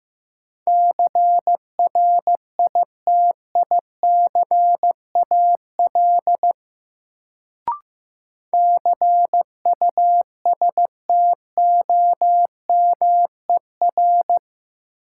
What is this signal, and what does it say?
Morse code, heard as a keyed tone, about 15 wpm, 700 Hz. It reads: CRITICAL CUSTOMER
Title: Sentences from Top 1100 Words - New Word Drill - 15wpm